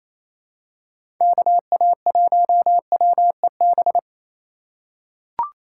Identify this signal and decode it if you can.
Morse code, heard as a keyed tone, about 28 wpm, 700 Hz. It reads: KA1WEB
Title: USA Call Signs - 6 Characters - 2x Character Spacing - 28wpm